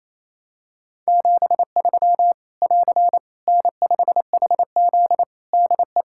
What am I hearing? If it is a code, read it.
Morse code, heard as a keyed tone, about 28 wpm, 700 Hz. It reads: 73 <AR> N5HZ DE
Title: Practice QSO - Exchange 4 - Ending - 28wpm